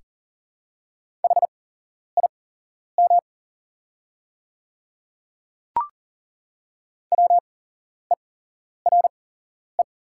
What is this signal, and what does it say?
Morse code, heard as a keyed tone, about 39 wpm, spaced as for 11 wpm, 690 Hz. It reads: HIM WERE
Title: 8x Character Spacing - Top 100 Words - 40wpm